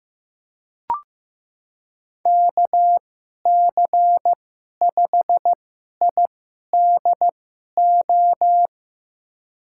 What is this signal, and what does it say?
Morse code, heard as a keyed tone, about 15 wpm, spaced as for 11 wpm, 700 Hz. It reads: KC5IDO
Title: USA Call Signs - 2x Character Spacing - 15wpm